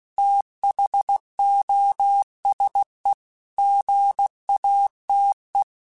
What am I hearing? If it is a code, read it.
Morse code, heard as a keyed tone, about 16 wpm, 800 Hz. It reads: THOSE GATE
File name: Technical Subjects / Technical Subjects-mp3s / AND Gate @15